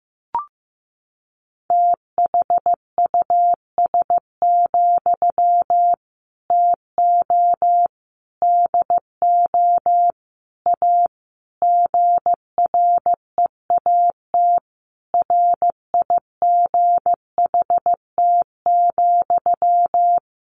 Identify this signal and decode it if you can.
Morse code, heard as a keyed tone, about 15 wpm, 700 Hz. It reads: THUS, TO DO A GREAT RIGHT,
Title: Sentences from Top 900 Words 15wpm